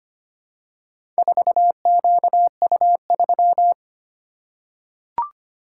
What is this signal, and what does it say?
Morse code, heard as a keyed tone, about 25 wpm, 700 Hz. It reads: 4QU3
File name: practice / 25wpm / Four Letter-Number Combinations - 25wpm